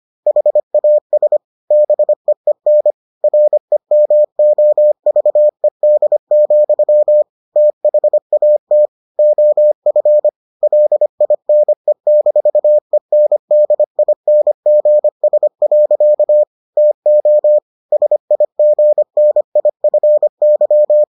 Morse code, heard as a keyed tone, about 25 wpm, 600 Hz. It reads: HAS BEEN REMOVED, THAT OF LINE-ENDINGS. TO SIGNIFY